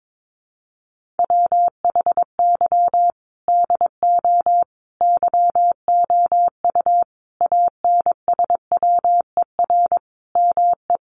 Morse code, heard as a keyed tone, about 22 wpm, 700 Hz. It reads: WHY DO YOU ANSWER ME
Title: Sentences from Top 1500 Words 22wpm